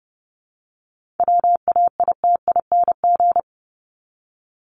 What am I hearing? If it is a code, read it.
Morse code, heard as a keyed tone, about 30 wpm, 700 Hz. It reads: WAITING
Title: ING Suffix - Top 100 Words with Base Words 30wpm